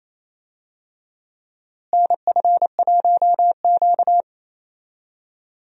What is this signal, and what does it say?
Morse code, heard as a keyed tone, about 27 wpm, 700 Hz. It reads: NF1Q
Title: USA Call Signs - 4 Characters - Repeated 3x - 28wpm